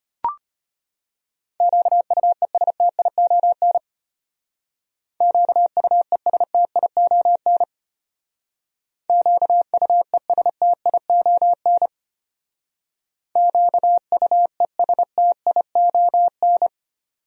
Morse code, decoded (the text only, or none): QUESTION QUESTION QUESTION QUESTION